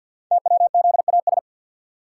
Morse code, heard as a keyed tone, about 48 wpm, 700 Hz. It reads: TWZAS
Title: Code Groups - 5 Letters - 50wpm